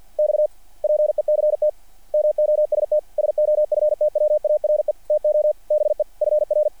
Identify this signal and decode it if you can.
Morse code, heard as a keyed tone, about 48 wpm, 600 Hz. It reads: XPECT MOST SOFTWARE TO BE FR
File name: archived / CRSnet